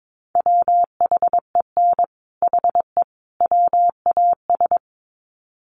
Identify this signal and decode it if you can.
Morse code, heard as a keyed tone, about 22 wpm, 700 Hz. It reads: WHEN HE WAS
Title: Sets of 3 Words 22wpm